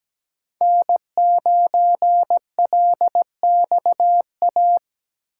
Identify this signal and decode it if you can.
Morse code, heard as a keyed tone, about 17 wpm, 700 Hz. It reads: N9LXA
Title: USA Call Signs - 5 Characters - 3x Character Spacing - 17wpm